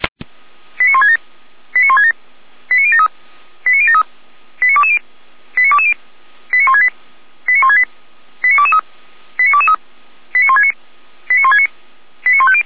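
Five-tone selective call (ZVEI), heard as quick runs of five tones, tones of about 70 ms. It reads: 89167 89167 89073 89073 89220 89220 89168 89168 89223 89223 89169 89169 89150
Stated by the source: Five-tone ZVEI selective call received by SDR, archived by the Orlamünde fire brigade (feuerwehr-orlamuende.org).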